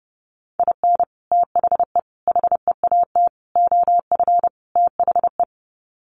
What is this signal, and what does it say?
Morse code, heard as a keyed tone, about 30 wpm, 700 Hz. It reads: IN THE HEAT OF THE
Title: Sets of 5 Words Encore2 30wpm